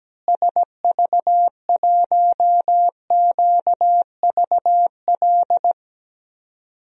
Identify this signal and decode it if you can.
Morse code, heard as a keyed tone, about 17 wpm, 700 Hz. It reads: SV1QVL
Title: DXCC-100 Call Signs - 6 Characters - Repeated 2x - 17wpm